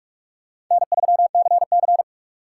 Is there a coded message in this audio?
Yes: Morse code from 0.7 to 2.0 s, heard as a keyed tone, about 42 wpm, 700 Hz. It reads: N3CC